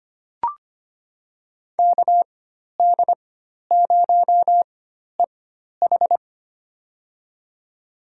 Morse code, heard as a keyed tone, about 25 wpm, spaced as for 12 wpm, 700 Hz. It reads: KD0EH
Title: USA Call Signs - 4x Character Spacing - 25wpm